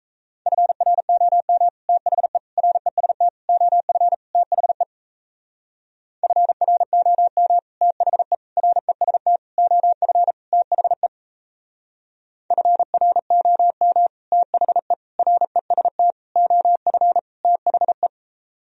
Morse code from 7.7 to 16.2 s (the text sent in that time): THE REST OF THE FROM THE REST